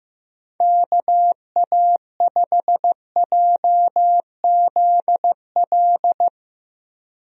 Morse code, heard as a keyed tone, about 15 wpm, 700 Hz. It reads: KA5JZL
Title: USA Call Signs - 6 Characters - Repeated 2x - 15wpm